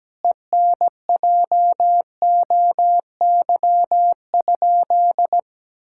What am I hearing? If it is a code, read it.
Morse code, heard as a keyed tone, about 17 wpm, 700 Hz. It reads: ENJOY?
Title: Sentences from Top 1000 Words - Review Part 5 - 17wpm